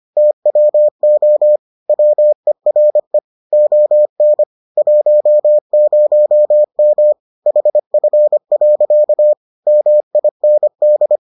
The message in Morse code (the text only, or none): TWO WERE ON 10M HF. MIND